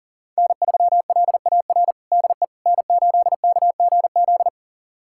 Morse code, heard as a keyed tone, about 40 wpm, 700 Hz. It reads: N3LAR DE N8KGZ
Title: USA Callsign DE Callsign - 5 Characters - 40wpm